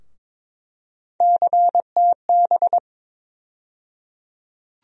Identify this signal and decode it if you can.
Morse code, heard as a keyed tone, about 22 wpm, 700 Hz. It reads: CTB
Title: USA Call Sign Postfixes - 3 Letters 22wpm